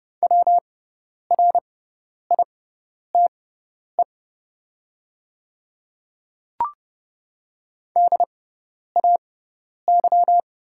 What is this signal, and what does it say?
Morse code, heard as a keyed tone, about 30 wpm, spaced as for 10 wpm, 700 Hz. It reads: WRITE DAY
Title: 6x Character Spacing - Top 100 Words - 30wpm